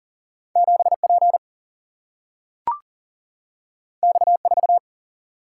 Morse code, heard as a keyed tone, about 40 wpm, 700 Hz. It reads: 7P X4
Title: Single-Letter Single-Number 40wpm